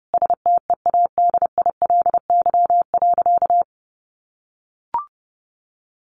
Morse code, heard as a keyed tone, about 30 wpm, 700 Hz. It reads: STEADILY.
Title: A Sound Of Thunder 30wpm